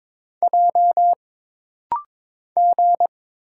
Morse code, heard as a keyed tone, about 22 wpm, 700 Hz. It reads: J G
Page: This rapid-fire timing is geared to push your ICR (Instant Character Recognition) skills.